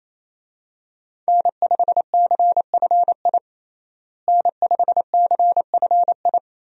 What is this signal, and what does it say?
Morse code, heard as a keyed tone, about 28 wpm, 700 Hz. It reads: N5CFI N5CFI
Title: USA Call Signs - 5 Characters - Repeated 3x - 28wpm